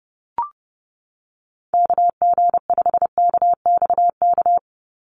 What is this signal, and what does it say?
Morse code, heard as a keyed tone, about 30 wpm, 700 Hz. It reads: KG5KXK
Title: USA Call Signs 30wpm